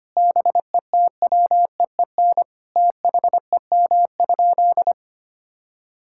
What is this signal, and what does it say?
Morse code, heard as a keyed tone, about 25 wpm, 700 Hz. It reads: BETWEEN THEM?
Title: Sentences from Top 1000 Words - Review Part 6 - 25wpm